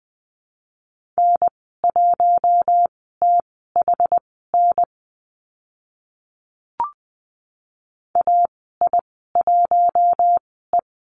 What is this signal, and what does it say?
Morse code, heard as a keyed tone, about 20 wpm, spaced as for 14 wpm, 700 Hz. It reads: N1THN AI1E
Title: 082 Lesson 14g - Number 2 - Call Signs 20wpm